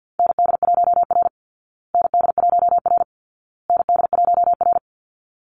Morse code, heard as a keyed tone, about 50 wpm, 700 Hz. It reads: ND1R ND1R ND1R
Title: Call Signs Repeated 2x - 50wpm